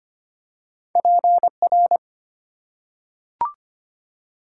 Morse code, heard as a keyed tone, about 25 wpm, 700 Hz. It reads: PR
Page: morse-code-ninja-practice-25wpm